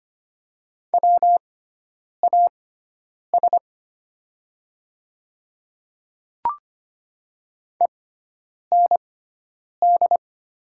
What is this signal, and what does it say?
Morse code, heard as a keyed tone, about 25 wpm, spaced as for 9 wpm, 700 Hz. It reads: WAS END
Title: 6x Character Spacing - Top 200 Words - 25wpm